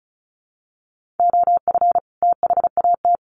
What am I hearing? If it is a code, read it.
Morse code, heard as a keyed tone, about 34 wpm, 700 Hz. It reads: OF THAT
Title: Sets of 2 Words 35wpm